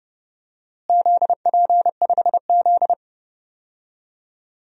Morse code, heard as a keyed tone, about 30 wpm, 700 Hz. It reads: ZP5Z
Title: DXCC-100 Call Signs - 4 Characters - Repeated 3x - 30wpm